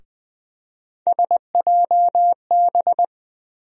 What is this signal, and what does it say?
Morse code, heard as a keyed tone, about 20 wpm, 700 Hz. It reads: SJB
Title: USA Call Sign Postfixes - 3 Letters 20wpm